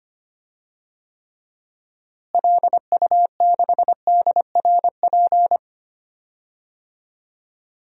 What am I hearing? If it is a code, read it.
Morse code, heard as a keyed tone, about 25 wpm, 700 Hz. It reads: LU6DRP
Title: DXCC-100 Call Signs - 6 Characters - Repeated 3x - 25wpm